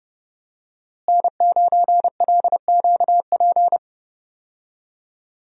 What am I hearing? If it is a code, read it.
Morse code, heard as a keyed tone, about 32 wpm, 700 Hz. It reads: N9LQP
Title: USA Call Signs - 5 Characters - Repeated 3x - 30wpm